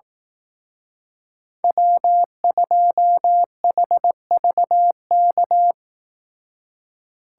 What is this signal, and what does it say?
Morse code, heard as a keyed tone, about 18 wpm, 700 Hz. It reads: W2HVK
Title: Speed Racing - USA 1x3 Call Signs 15wpm